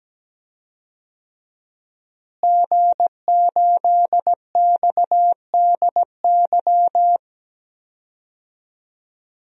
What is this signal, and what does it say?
Morse code, heard as a keyed tone, about 17 wpm, 700 Hz. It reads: G8XDY
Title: DXCC-100 Call Signs - 5 Characters - Repeated 3x - 17wpm